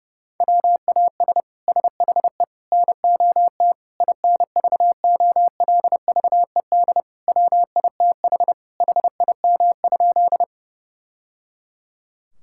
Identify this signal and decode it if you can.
Morse code, heard as a keyed tone, about 32 wpm, 700 Hz. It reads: WAS SHE NOT INVOLVED WITH HIM?
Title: Sentences from Top 1000 Words - Review Part 5 - 30wpm